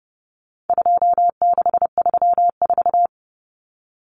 Morse code, heard as a keyed tone, about 30 wpm, 700 Hz. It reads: 2634
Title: Numbers - 4-Digit 30wpm